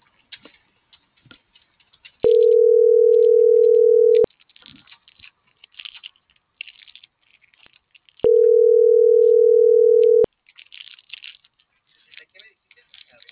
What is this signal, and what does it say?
Ringback tone (North America), 440 + 480 Hz.